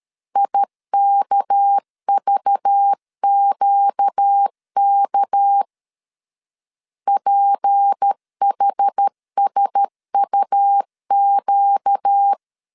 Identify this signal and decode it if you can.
Morse code, heard as a keyed tone, about 13 wpm, 800 Hz. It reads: IKVQK PHSUQ
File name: LEZIONI / 6 PKX0 / 6 PK 12wpm 2006